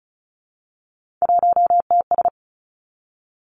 Morse code, heard as a keyed tone, about 36 wpm, 700 Hz. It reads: 1TS
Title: Three Challenging Letter-Number Combinations 35wpm